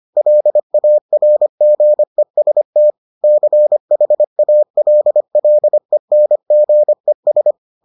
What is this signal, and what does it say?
Morse code, heard as a keyed tone, about 25 wpm, 600 Hz. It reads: LARGEST CHALLENGES